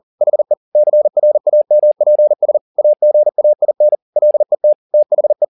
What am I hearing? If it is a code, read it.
Morse code, heard as a keyed tone, about 40 wpm, 600 Hz. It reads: HE CRAMPS AGAIN LET THE
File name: the old man and the sea 2 40wpm